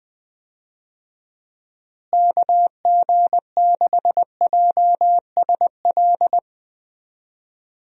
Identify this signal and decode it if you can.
Morse code, heard as a keyed tone, about 20 wpm, 700 Hz. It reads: KG6JSL
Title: USA Call Signs - 6 Characters - Repeated 3x - 20wpm